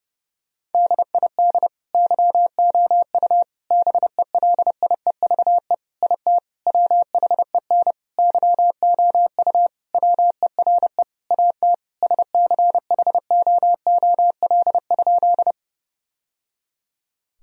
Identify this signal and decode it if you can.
Morse code, heard as a keyed tone, about 30 wpm, 700 Hz. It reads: DID YOU BELIEVE IT WHEN YOU WERE AT SCHOOL?
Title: Sentences from Top 1000 Words - Review Part 1 - 30wpm